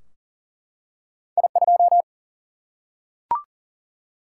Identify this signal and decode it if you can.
Morse code, heard as a keyed tone, about 38 wpm, 700 Hz. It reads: I2
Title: Single-Letter Single-Number 40wpm